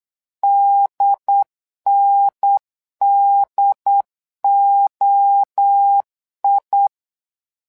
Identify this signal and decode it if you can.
Morse code, heard as a keyed tone, about 8 wpm, 800 Hz. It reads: DNDOI
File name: LEZIONI / 3 MWRO / 3 MWRO 8wpm 2606